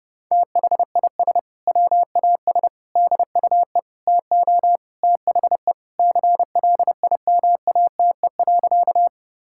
Morse code, heard as a keyed tone, about 32 wpm, 700 Hz. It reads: THIS WAS DUE TO THE CLIMATE.
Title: Sentences from Top 1600 Words 30wpm